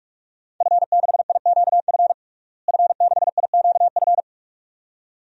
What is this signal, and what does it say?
Morse code, heard as a keyed tone, about 41 wpm, 700 Hz. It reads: F6IQF F6IQF
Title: DXCC-100 Call Signs - 5 Characters - Repeated 2x - 45wpm